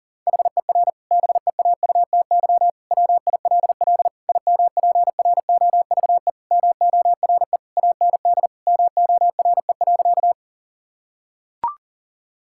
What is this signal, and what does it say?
Morse code, heard as a keyed tone, about 40 wpm, 700 Hz. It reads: HER BEAUTY WILL IMPROVE MORE AND MORE.
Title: Sentences from Top 1000 Words - Review Part 5 - 40wpm